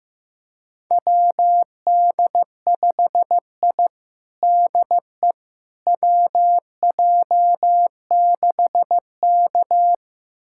Morse code, heard as a keyed tone, about 16 wpm, 700 Hz. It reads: WD5I DE WJ6K
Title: USA Callsign DE Callsign - 4 Characters - 15wpm